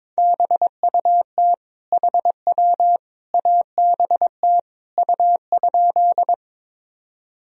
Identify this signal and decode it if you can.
Morse code, heard as a keyed tone, about 22 wpm, 700 Hz. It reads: BUT HW ABT U?
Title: Sentences from Top 100 Words and Abbreviations 22wpm